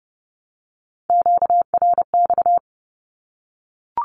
Morse code, heard as a keyed tone, about 30 wpm, 700 Hz. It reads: QRX
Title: Q-Codes Repeated 3x 30wpm